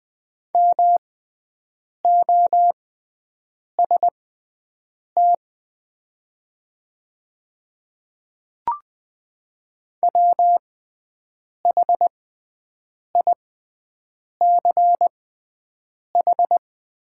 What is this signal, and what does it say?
Morse code, heard as a keyed tone, about 20 wpm, spaced as for 7 wpm, 700 Hz. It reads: MOST WHICH
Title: 6x Character Spacing - Top 100 Words - 20wpm